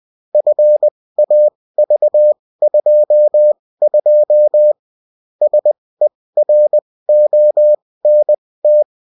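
Morse code, heard as a keyed tone, about 20 wpm, spaced as for 16 wpm, 600 Hz. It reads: FAV22 SERONT
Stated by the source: Bulletin in text and CW form.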